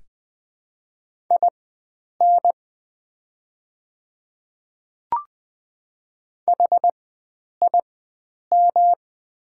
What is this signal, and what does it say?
Morse code, heard as a keyed tone, about 20 wpm, spaced as for 9 wpm, 690 Hz. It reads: IN HIM
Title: 4x Character Spacing - Top 100 Words - 20wpm